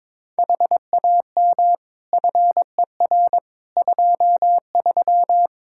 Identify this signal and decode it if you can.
Morse code, heard as a keyed tone, about 22 wpm, 700 Hz. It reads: HAM FER 23
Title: Practice QSO - Exchange 3 - More Info - 22wpm